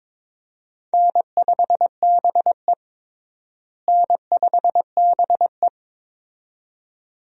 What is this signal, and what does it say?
Morse code, heard as a keyed tone, about 22 wpm, 700 Hz. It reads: N5BE N5BE
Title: USA Call Signs - 4 Characters - Repeated 2x - 22wpm